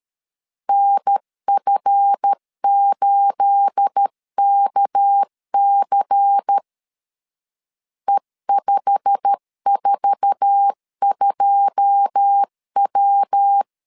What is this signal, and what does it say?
Morse code, heard as a keyed tone, about 13 wpm, 800 Hz. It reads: NF8KC E542W
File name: LEZIONI / 9 28 12wpm 2351